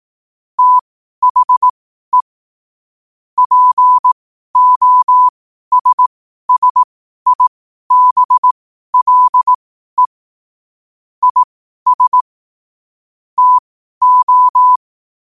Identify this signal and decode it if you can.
Morse code, heard as a keyed tone, about 18 wpm, spaced as for 12 wpm, 1000 Hz. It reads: THE POSSIBLE IS TO
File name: qotdfiles / 12wpm